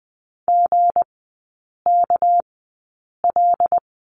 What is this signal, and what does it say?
Morse code, heard as a keyed tone, about 20 wpm, spaced as for 8 wpm, 700 Hz. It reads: GKL
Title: 198 Lesson 32e - Letter k - Three Characters 20wpm